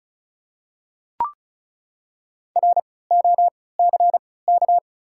Morse code, heard as a keyed tone, about 36 wpm, spaced as for 20 wpm, 690 Hz. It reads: ROCK
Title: 3x Character Spacing - Top 500 Words - 35wpm